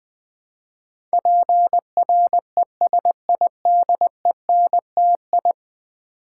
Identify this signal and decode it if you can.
Morse code, heard as a keyed tone, about 20 wpm, 700 Hz. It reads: PRESIDENTI
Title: Build A Word - 5 Letters At A Time - 12-Letter Words - 20wpm